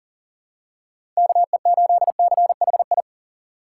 Morse code, heard as a keyed tone, about 43 wpm, 700 Hz. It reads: KE8CHI
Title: USA Call Signs - 4x Character Spacing - 40wpm